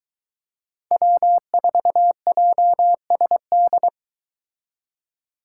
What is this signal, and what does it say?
Morse code, heard as a keyed tone, about 23 wpm, 700 Hz. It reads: W4JSD